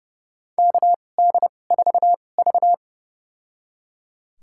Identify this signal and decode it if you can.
Morse code, heard as a keyed tone, about 29 wpm, spaced as for 21 wpm, 700 Hz. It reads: KD4V
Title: USA Call Signs - 4 Characters - 2x Character Spacing - 30wpm